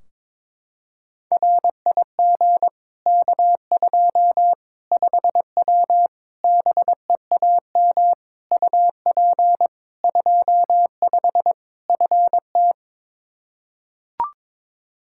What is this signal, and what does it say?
Morse code, heard as a keyed tone, about 22 wpm, 700 Hz. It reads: RIG K2 5W BEAM UP 25 FT
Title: Practice QSO - Exchange 2 - Info - 22wpm